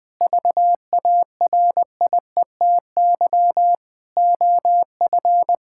Morse code, heard as a keyed tone, about 21 wpm, 700 Hz. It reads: VARIETY OF
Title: Sentences with 4-Word Phrases - Spoken Beginning - 20wpm